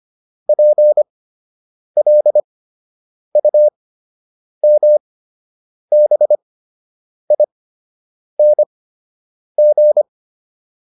Morse code, heard as a keyed tone, about 25 wpm, spaced as for 8 wpm, 600 Hz. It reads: PLUMBING